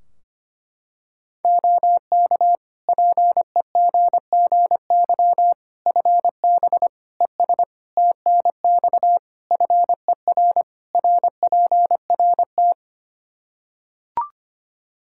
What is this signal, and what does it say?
Morse code, heard as a keyed tone, about 25 wpm, 700 Hz. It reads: OK PEGGY FB ES TNX FER RPRT
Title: Practice QSO - Exchange 2 - Info - 25wpm